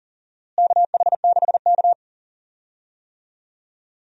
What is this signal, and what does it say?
Morse code, heard as a keyed tone, about 40 wpm, 700 Hz. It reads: KH6K
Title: DXCC-100 Call Signs - 4 Characters - Repeated 2x - 40wpm